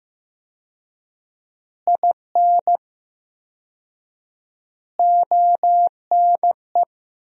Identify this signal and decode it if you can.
Morse code, heard as a keyed tone, about 16 wpm, 700 Hz. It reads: IN ONE
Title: 4x Word-Spacing - Sentences from Top 500 Words - 15wpm